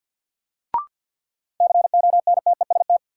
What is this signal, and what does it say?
Morse code, heard as a keyed tone, about 52 wpm, 700 Hz. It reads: CONTEST
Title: Sentences from Top 1600 Words - New Word Drill - 50wpm